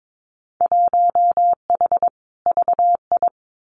Morse code, heard as a keyed tone, about 22 wpm, 700 Hz. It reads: 1H VI
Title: Winter Field Day Exchanges Encore 22wpm